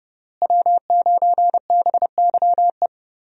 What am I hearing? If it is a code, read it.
Morse code, heard as a keyed tone, about 31 wpm, 700 Hz. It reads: W9BYE
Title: USA Call Signs - 5 Characters - Repeated 3x - 30wpm